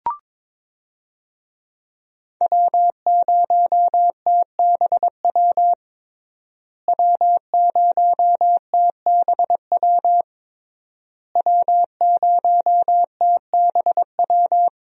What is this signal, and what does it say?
Morse code, heard as a keyed tone, about 22 wpm, 700 Hz. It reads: W0TBW W0TBW W0TBW
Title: USA Call Signs - 5 Characters - Repeated 3x - 22wpm